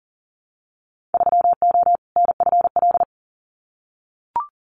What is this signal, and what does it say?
Morse code, heard as a keyed tone, about 40 wpm, 700 Hz. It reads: 3O NFL
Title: Winter Field Day Exchanges 40wpm